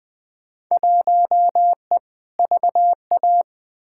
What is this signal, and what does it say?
Morse code, heard as a keyed tone, about 20 wpm, 700 Hz. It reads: 1E VA
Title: Field Day Exchanges - 20wpm